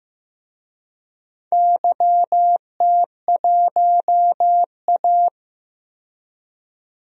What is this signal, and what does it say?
Morse code, heard as a keyed tone, about 15 wpm, 700 Hz. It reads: YT1A
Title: DXCC-100 Call Signs - 4 Characters - 15wpm